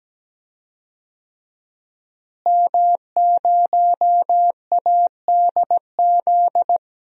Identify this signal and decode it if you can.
Morse code, heard as a keyed tone, about 17 wpm, 700 Hz. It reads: M0ADZ
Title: DXCC-100 Call Signs - 5 Characters - Repeated 3x - 17wpm